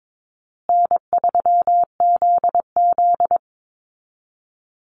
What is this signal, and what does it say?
Morse code, heard as a keyed tone, about 22 wpm, 700 Hz. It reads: N3ZZ
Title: Easy Call Signs Repeated 3x - 22wpm